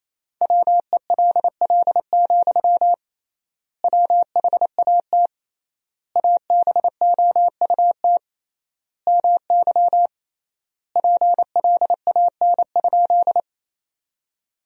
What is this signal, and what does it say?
Morse code, heard as a keyed tone, about 28 wpm, 700 Hz. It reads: WELL, WHAT ABOUT MY PLAN?
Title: 3x Word-Spacing - Sentences from Top 500 Words - 28wpm